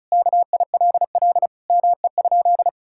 Morse code, heard as a keyed tone, about 36 wpm, 700 Hz. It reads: KILL ME?
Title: Sentences from Top 1600 Words 35wpm